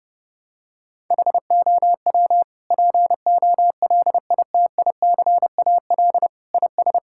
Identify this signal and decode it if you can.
Morse code, heard as a keyed tone, about 30 wpm, 700 Hz. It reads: HOW POLITICAL IS
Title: Sentences from Top 1000 Words - Review Part 5 - 30wpm